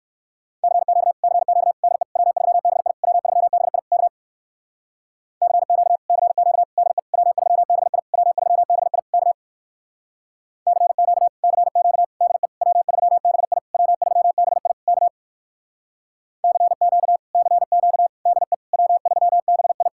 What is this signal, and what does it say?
Morse code, heard as a keyed tone, about 59 wpm, 700 Hz. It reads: CQ CQ DE W3BI W3BI K CQ CQ DE W3BI W3BI K CQ CQ DE W3BI W3BI K CQ CQ DE W3BI